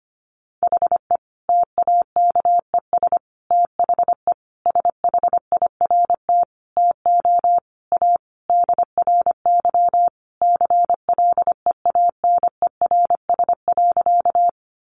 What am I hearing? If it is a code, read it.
Morse code, heard as a keyed tone, about 25 wpm, 700 Hz. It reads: HE TAKES THE SHIRT TO A DRY CLEANERS.
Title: Additional Easy Phrases 25wpm